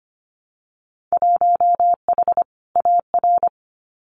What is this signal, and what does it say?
Morse code, heard as a keyed tone, about 25 wpm, 700 Hz. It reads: 1H AR
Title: Winter Field Day Exchanges 25wpm